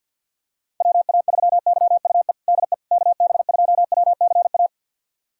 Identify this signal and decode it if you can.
Morse code, heard as a keyed tone, about 52 wpm, 700 Hz. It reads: WA3YUE DE KB2WCA
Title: USA Callsign DE Callsign 50wpm